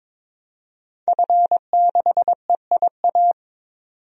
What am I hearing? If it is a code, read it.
Morse code, heard as a keyed tone, about 22 wpm, 700 Hz. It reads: F6EIA
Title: DXCC-100 Call Signs - 5 Characters - 22wpm